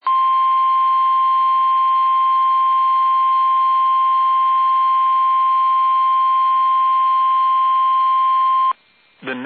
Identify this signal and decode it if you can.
Emergency Alert System (SAME) tones (the 1050 Hz attention tone). It is heard from 0.0 to 8.8 s.
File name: eas / KUSO / KUSO